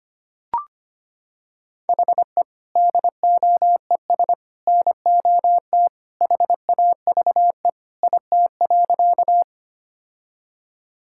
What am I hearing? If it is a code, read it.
Morse code, heard as a keyed tone, about 25 wpm, 700 Hz. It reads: HE DOES NOT HAVE IT.
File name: practice / 25wpm / Sentences with 3-Word Phrases - 25wpm